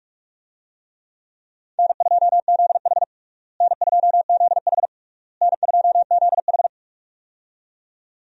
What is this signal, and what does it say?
Morse code, heard as a keyed tone, about 42 wpm, 700 Hz. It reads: N2ZH N2ZH N2ZH
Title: USA Call Signs - 4 Characters - Repeated 3x - 45wpm